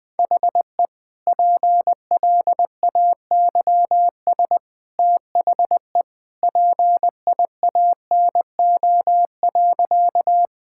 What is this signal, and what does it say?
Morse code, heard as a keyed tone, about 20 wpm, 700 Hz. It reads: HE PLAYS THE PIANO.
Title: Easy Sentences with Hints - 20wpm